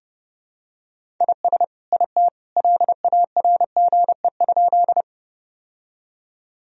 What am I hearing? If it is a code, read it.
Morse code, heard as a keyed tone, about 32 wpm, 700 Hz. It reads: IS IT LARGE?
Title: Sentences from Top 1000 Words - Review Part 1 - 30wpm